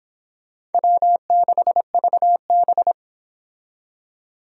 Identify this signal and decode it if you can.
Morse code, heard as a keyed tone, about 32 wpm, 700 Hz. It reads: W6VB